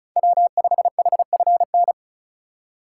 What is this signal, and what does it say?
Morse code, heard as a keyed tone, about 36 wpm, 700 Hz. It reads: W5HFN
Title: USA Call Signs - 5 Characters - 2x Character Spacing - 35wpm